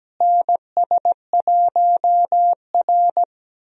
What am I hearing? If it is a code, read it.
Morse code, heard as a keyed tone, about 17 wpm, 700 Hz. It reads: NS1R
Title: USA Call Signs - 4 Characters - Repeated 3x - 17wpm